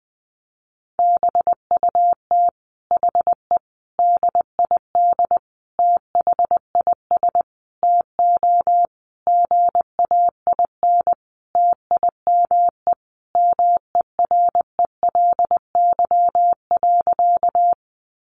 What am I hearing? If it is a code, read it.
Morse code, heard as a keyed tone, about 20 wpm, 700 Hz. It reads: BUT HE DID THIS TO GAIN TIME MERELY.
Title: Sentences from Top 1400 Words 20wpm